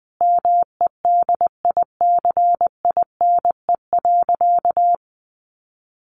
Morse code, heard as a keyed tone, about 20 wpm, 700 Hz. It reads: MEDICINE.
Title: Sentences from Top 900 Words 20wpm